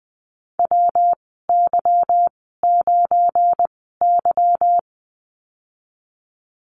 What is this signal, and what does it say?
Morse code, heard as a keyed tone, about 20 wpm, spaced as for 15 wpm, 700 Hz. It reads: WY9Y
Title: 179 Lesson 29g - Slash - Call Signs 20wpm